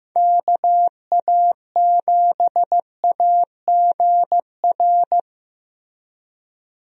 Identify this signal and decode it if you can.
Morse code, heard as a keyed tone, about 15 wpm, 700 Hz. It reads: KA7AGR
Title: USA Call Signs - 6 Characters - Repeated 2x - 15wpm